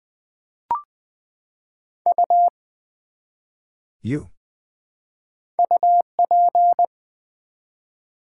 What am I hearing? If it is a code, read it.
Morse code, heard as a keyed tone, about 20 wpm, 700 Hz. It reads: U UP
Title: Build A Word - 1 Letter At A Time - 5-Letter Words - 20wpm